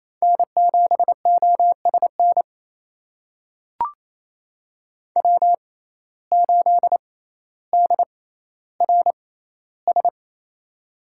Morse code, heard as a keyed tone, about 27 wpm, spaced as for 14 wpm, 700 Hz. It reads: N7OSN W8DRS
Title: USA Call Signs - 5 Characters - 6x Character Spacing - 28wpm